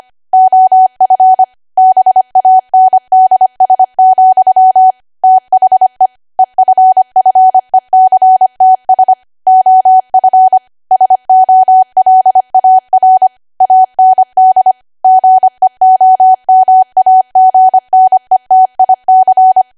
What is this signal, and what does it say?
Morse code, heard as a keyed tone, about 25 wpm, 750 Hz. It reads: OF BANDS, THE EFFECTS OF SOLAR AND GEOMAGNETIC
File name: cwa / W1AW-Level3 / 25WPM